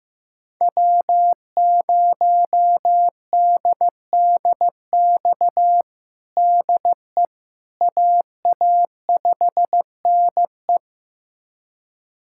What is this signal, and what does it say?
Morse code, heard as a keyed tone, about 15 wpm, 700 Hz. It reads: W0DDX DE AA5NE
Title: USA Callsign DE Callsign - 5 Characters - 15wpm